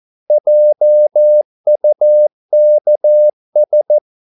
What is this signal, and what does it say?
Morse code, heard as a keyed tone, about 14 wpm, 600 Hz. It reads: JUKS
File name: Korte ord DK 14wpm